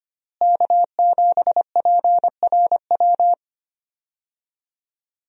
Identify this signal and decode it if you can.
Morse code, heard as a keyed tone, about 25 wpm, 700 Hz. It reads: K7PRW
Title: USA Call Signs - 5 Characters - Repeated 2x - 25wpm